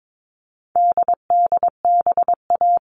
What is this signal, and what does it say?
Morse code, heard as a keyed tone, about 22 wpm, 700 Hz. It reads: DDBA
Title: All Cut Numbers - 4-Digit 22wpm